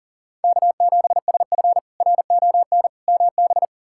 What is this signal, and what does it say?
Morse code, heard as a keyed tone, about 40 wpm, 700 Hz. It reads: K7SF RON MB
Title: K1USN SST QSO Practice - 40wpm